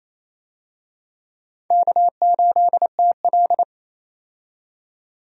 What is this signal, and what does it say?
Morse code, heard as a keyed tone, about 28 wpm, 700 Hz. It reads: K8TL
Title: USA Call Signs - 4 Characters - 28wpm